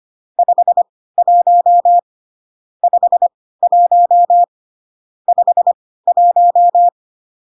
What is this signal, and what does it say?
Morse code, heard as a keyed tone, about 25 wpm, spaced as for 16 wpm, 700 Hz. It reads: 51 51 51